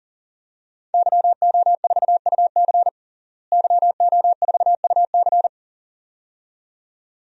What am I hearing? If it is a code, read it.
Morse code, heard as a keyed tone, about 40 wpm, 700 Hz. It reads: YO4UC YO4UC
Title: DXCC-100 Call Signs - 5 Characters - Repeated 2x - 40wpm